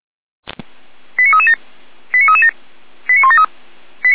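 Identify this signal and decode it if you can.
Five-tone selective call (ZVEI), heard as quick runs of five tones, tones of about 70 ms. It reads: 89337 89337 89173
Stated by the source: ZVEI five-tone selective call received on an SDR.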